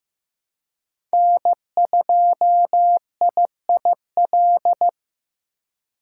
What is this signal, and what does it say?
Morse code, heard as a keyed tone, about 16 wpm, 700 Hz. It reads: N2IIL
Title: USA Call Signs - 5 Characters - 4x Character Spacing - 15wpm